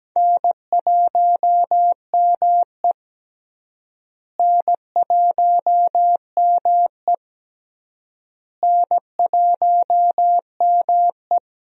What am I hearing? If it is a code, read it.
Morse code, heard as a keyed tone, about 17 wpm, 700 Hz. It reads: N1ME N1ME N1ME
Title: USA Call Signs - 4 Characters - Repeated 3x - 17wpm